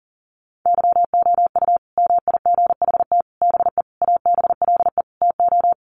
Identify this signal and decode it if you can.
Morse code, heard as a keyed tone, about 40 wpm, 700 Hz. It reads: YOU MIGHT BE ABLE TO
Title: Sets of 5 Words Encore2 40wpm